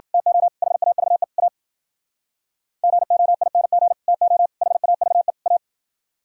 Morse code, heard as a keyed tone, about 49 wpm, 700 Hz. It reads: TO HAVE A GOING TO HAVE A
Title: Speed-Racing - Sets of 4 Words 40wpm